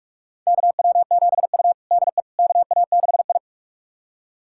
Morse code, heard as a keyed tone, about 45 wpm, 700 Hz. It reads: KW7U DE KA6I
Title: USA Callsign DE Callsign - 4 Characters - 45wpm